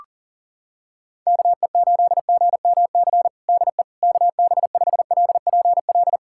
Morse code, heard as a keyed tone, about 37 wpm, 700 Hz. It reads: KE8GMC DE KB5LPL
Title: USA Callsign DE Callsign - 6 Characters - 40wpm